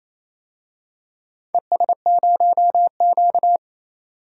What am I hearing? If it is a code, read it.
Morse code, heard as a keyed tone, about 27 wpm, 690 Hz. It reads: ES0Q
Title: DXCC-100 Call Signs - 4 Characters - 28wpm